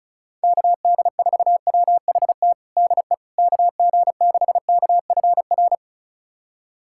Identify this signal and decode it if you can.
Morse code, heard as a keyed tone, about 34 wpm, 700 Hz. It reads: KD4WHT DE KG6KFR
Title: USA Callsign DE Callsign - 6 Characters - 35wpm